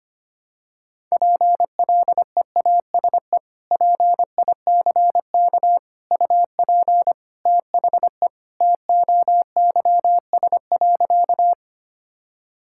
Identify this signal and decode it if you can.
Morse code, heard as a keyed tone, about 25 wpm, 700 Hz. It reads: PLEASE PICK UP THE TOYS.
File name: practice / 25wpm / Sentences with 2-Word Phrases - 25wpm